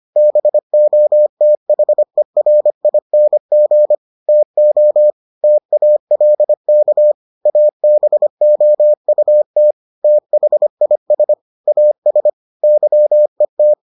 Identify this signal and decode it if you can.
Morse code, heard as a keyed tone, about 25 wpm, 600 Hz. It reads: BOTHERING TO TALK ABOUT THIS AS YET